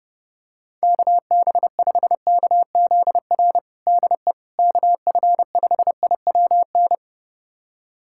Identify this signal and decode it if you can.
Morse code, heard as a keyed tone, about 30 wpm, 700 Hz. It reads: KB5KZR DE KF5IWN
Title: USA Callsign DE Callsign - 6 Characters - 30wpm